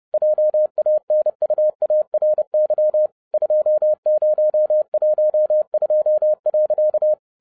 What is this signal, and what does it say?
Morse code, heard as a keyed tone, about 30 wpm, 610 Hz. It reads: JANUARY 2012.
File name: 30wpm